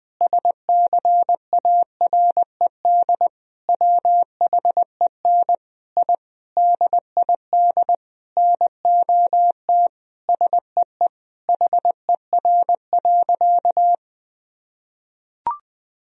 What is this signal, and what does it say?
Morse code, heard as a keyed tone, about 20 wpm, 700 Hz. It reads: SCARED WHEN I DID NOT SEE HER.
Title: Sentences from Top 1000 Words - Review Part 4 - 20wpm